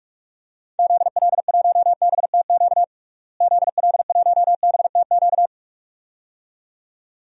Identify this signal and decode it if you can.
Morse code, heard as a keyed tone, about 46 wpm, 700 Hz. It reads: ZL1BTQ ZL1BTQ
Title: DXCC-100 Call Signs - 6 Characters - Repeated 2x - 45wpm